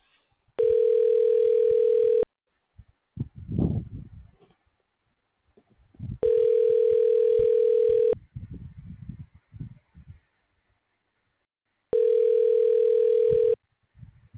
Ringback tone (North America), 440 + 480 Hz.
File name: recordings / RE c39c37106b371f758